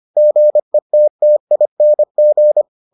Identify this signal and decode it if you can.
Morse code, heard as a keyed tone, about 25 wpm, 600 Hz. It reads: GETTING